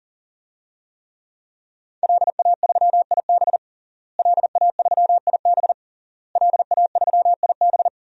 Morse code, heard as a keyed tone, about 37 wpm, 700 Hz. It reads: LA3IB LA3IB LA3IB
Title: DXCC-100 Call Signs - 5 Characters - Repeated 3x - 40wpm